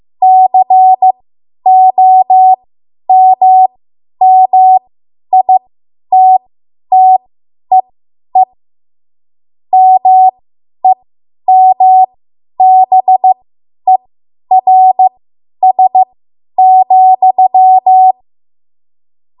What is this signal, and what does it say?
Morse code, heard as a keyed tone, about 15 wpm, spaced as for 10 wpm, 750 Hz. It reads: COMMITTEE MEMBERS,
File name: Morse / 10WPM